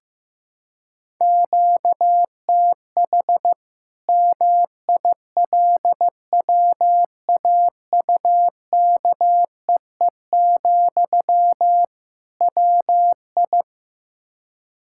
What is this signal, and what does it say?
Morse code, heard as a keyed tone, about 15 wpm, 700 Hz. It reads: QTH MILWAUKEE, WI
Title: Practice QSO - Exchange 1 - Beginning - 15wpm